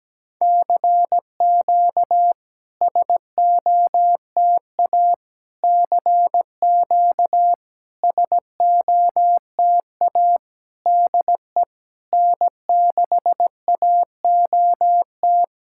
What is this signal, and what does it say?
Morse code, heard as a keyed tone, about 17 wpm, 700 Hz. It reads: CQ SOTA CQ SOTA DE N6AOT
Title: SOTA QSO Practice 17wpm